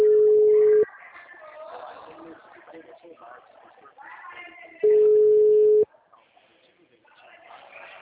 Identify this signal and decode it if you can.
Ringback tone (Europe, 425 Hz), with speech.